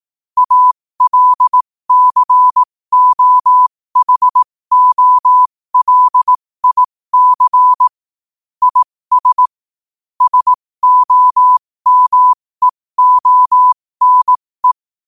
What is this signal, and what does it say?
Morse code, heard as a keyed tone, about 18 wpm, spaced as for 15 wpm, 1000 Hz. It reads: ALCOHOLIC IS SOMEONE
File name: qotdfiles / 15wpm